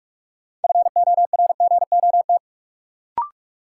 Morse code, heard as a keyed tone, about 42 wpm, 700 Hz. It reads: FORGOT
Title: Sentences from Top 1400 Words - New Word Drill - 45wpm